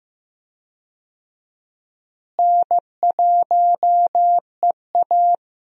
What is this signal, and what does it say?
Morse code, heard as a keyed tone, about 15 wpm, 700 Hz. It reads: N1EA
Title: USA Call Signs - 4 Characters - Repeated 2x - 15wpm